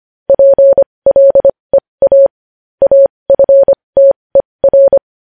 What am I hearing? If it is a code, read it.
Morse code, heard as a keyed tone, about 25 wpm, spaced as for 20 wpm, 550 Hz. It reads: PLEA AFTER
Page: News headlines in Morse code at 20 WPM